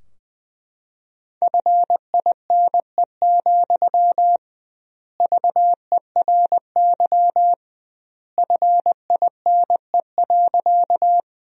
Morse code, heard as a keyed tone, about 20 wpm, 700 Hz. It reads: FINE, VERY FINE.
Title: 2x Word-Spacing - Sentences from Top 500 Words - 20wpm